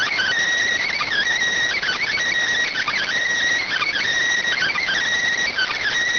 Emergency Alert System (SAME) tones (header bursts), heard as harsh buzzing bursts. It is heard from 0.7 to 5.0 s.